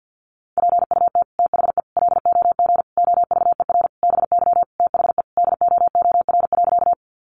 Morse code, heard as a keyed tone, about 50 wpm, 700 Hz. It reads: PUT THE LOG OVER BY THE DOOR.
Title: Sentences from Top 1400 Words 50wpm